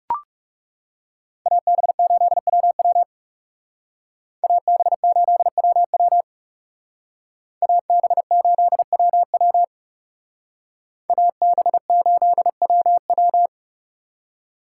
Morse code, heard as a keyed tone, about 38 wpm, 700 Hz. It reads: AB8WW AB8WW AB8WW AB8WW